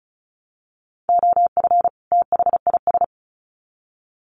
Morse code, heard as a keyed tone, about 34 wpm, 700 Hz. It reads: OF THIS
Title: Sets of 2 Words 35wpm